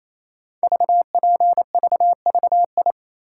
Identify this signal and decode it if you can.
Morse code, heard as a keyed tone, about 29 wpm, 700 Hz. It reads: VPVVI
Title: Code Groups - 5 Letters - 28wpm